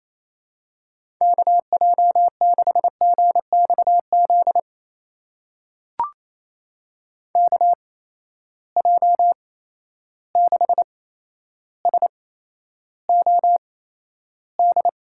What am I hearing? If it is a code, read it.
Morse code, heard as a keyed tone, about 28 wpm, spaced as for 12 wpm, 700 Hz. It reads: KJ6GXZ KJ6SOD
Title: USA Call Signs - 6 Characters - 8x Character Spacing - 28wpm